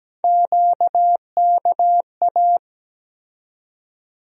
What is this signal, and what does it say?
Morse code, heard as a keyed tone, about 17 wpm, 700 Hz. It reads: QKA
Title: USA Call Sign Postfixes - 3 Letters 17wpm